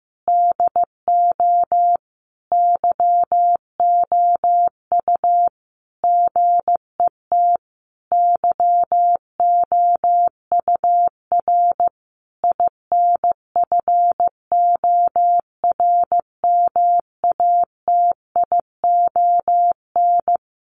Morse code, heard as a keyed tone, about 15 wpm, 700 Hz. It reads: DO YOU GET YOUR INFORMATION
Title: Sentences from Top 600 Words 15wpm